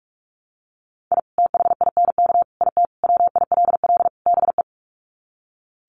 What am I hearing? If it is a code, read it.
Morse code, heard as a keyed tone, about 44 wpm, 700 Hz. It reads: I THINK IT WILL BE
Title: Sets of 5 Words Encore2 45wpm